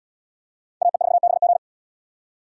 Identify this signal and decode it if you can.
Morse code, heard as a keyed tone, about 73 wpm, 700 Hz. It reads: AE4BG